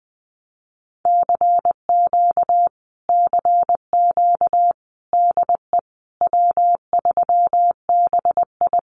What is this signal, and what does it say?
Morse code, heard as a keyed tone, about 20 wpm, 700 Hz. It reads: CQ CQ DE W3BI
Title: Common QSO Phrases Encore 20wpm